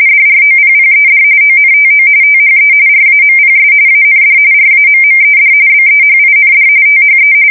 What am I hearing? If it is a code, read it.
RTTY, about 45.45 baud. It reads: DROPPED ONTO THE RAMPS AND RAMMED BEHIND THE
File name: fsk170 / battleshipnj